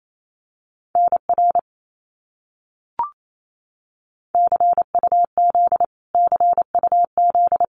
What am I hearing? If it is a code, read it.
Morse code, heard as a keyed tone, about 28 wpm, 700 Hz. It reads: NR CUZ CUZ
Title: Abbreviations Repeated 3x - 28wpm